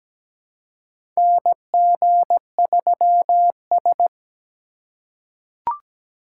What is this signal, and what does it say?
Morse code, heard as a keyed tone, about 17 wpm, 700 Hz. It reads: NG3S
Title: USA Call Signs - 2x Character Spacing - 17wpm